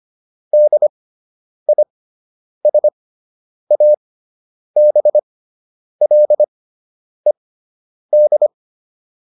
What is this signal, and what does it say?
Morse code, heard as a keyed tone, about 25 wpm, spaced as for 9 wpm, 600 Hz. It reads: DISABLED